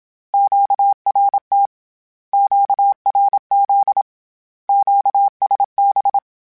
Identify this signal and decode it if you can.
Morse code, heard as a keyed tone, about 27 wpm, 800 Hz. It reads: QRT QRZ QSB
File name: Q Calls / Q Calls-mp3s / Q Calls Once @25